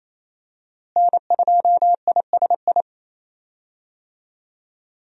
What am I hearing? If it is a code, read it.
Morse code, heard as a keyed tone, about 28 wpm, 700 Hz. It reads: N2ISI
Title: USA Call Signs - 5 Characters - Repeated 3x - 28wpm